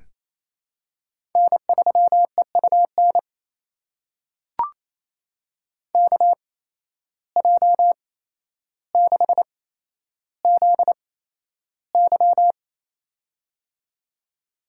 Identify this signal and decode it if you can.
Morse code, heard as a keyed tone, about 28 wpm, spaced as for 12 wpm, 700 Hz. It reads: N3EUN KJ6ZY
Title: USA Call Signs - 5 Characters - 8x Character Spacing - 28wpm